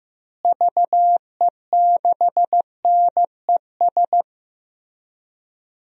Morse code, heard as a keyed tone, about 15 wpm, 700 Hz. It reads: VE6NES
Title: DXCC-100 Call Signs - 6 Characters - Repeated 3x - 15wpm